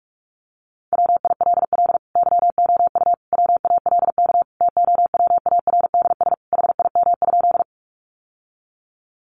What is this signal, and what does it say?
Morse code, heard as a keyed tone, about 44 wpm, 700 Hz. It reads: WILL YOU WALK TOWARDS HIM?
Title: Sentences from Top 1100 Words 45wpm